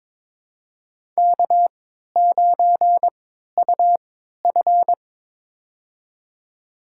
Morse code, heard as a keyed tone, about 22 wpm, spaced as for 13 wpm, 700 Hz. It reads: K9UF
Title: USA Call Signs - 3x Character Spacing - 22wpm